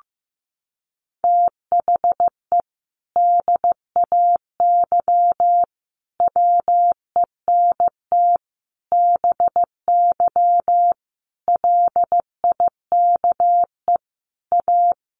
Morse code, heard as a keyed tone, about 15 wpm, 700 Hz. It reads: THE DAY WENT BY LIKE A
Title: Sentences from Top 1500 Words 15wpm